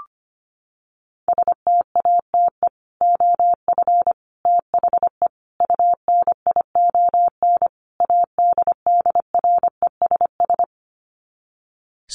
Morse code, heard as a keyed tone, about 25 wpm, 700 Hz. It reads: STATE OF THE UNION ADDRESS
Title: Sets of 5 Words Encore 25wpm